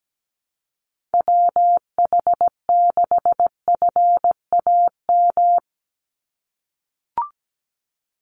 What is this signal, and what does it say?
Morse code, heard as a keyed tone, about 17 wpm, 700 Hz. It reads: WH6FAM
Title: USA Call Signs 17wpm